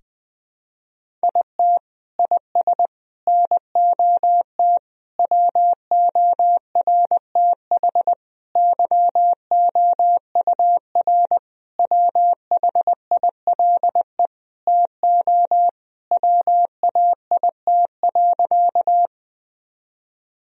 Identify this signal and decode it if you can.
Morse code, heard as a keyed tone, about 20 wpm, 700 Hz. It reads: IT IS NOT WORTH YOUR WHILE TO WAIT.
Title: Sentences from Top 1000 Words - Review Part 1 - 20wpm